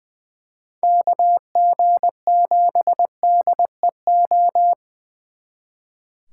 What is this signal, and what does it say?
Morse code, heard as a keyed tone, about 24 wpm, 700 Hz. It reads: KG7DEO
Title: Speed Racing - USA 2x3 Call Signs 20wpm